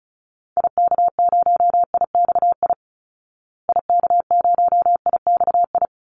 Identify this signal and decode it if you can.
Morse code, heard as a keyed tone, about 34 wpm, 700 Hz. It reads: IK0IXI IK0IXI
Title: Call Signs Repeated 3x - 35wpm